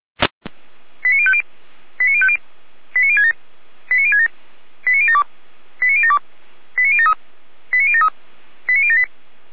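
Five-tone selective call (ZVEI), heard as quick runs of five tones, tones of about 70 ms. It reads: 89050 89050 89067 89067 89072 89072 89073 89073 89078
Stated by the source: Five-tone ZVEI selective call received by SDR, from the audio archive of Feuerwehr Orlamünde fire brigade.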